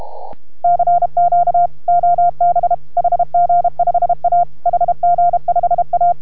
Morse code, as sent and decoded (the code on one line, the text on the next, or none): -.-. --.- / --- -... / .... --. ..... .- / .... --. ..... .-
CQ OB HG5A HG5A